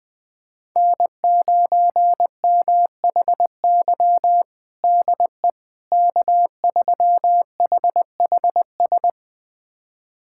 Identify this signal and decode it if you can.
Morse code, heard as a keyed tone, about 20 wpm, 700 Hz. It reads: N9MHY DE K3HHS
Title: USA Callsign DE Callsign - 5 Characters - 20wpm